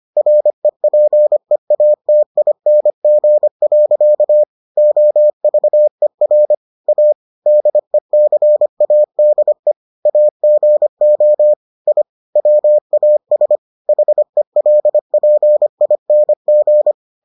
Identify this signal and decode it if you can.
Morse code, heard as a keyed tone, about 25 wpm, 600 Hz. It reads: REPEATING. OVER A DECADE AGO I WAS HELPING